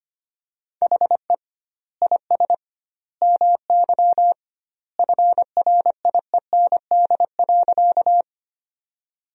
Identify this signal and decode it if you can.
Morse code, heard as a keyed tone, about 25 wpm, 700 Hz. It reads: HE IS MY FRIEND.
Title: 2x Word-Spacing - Sentences from Top 300 Words - 25wpm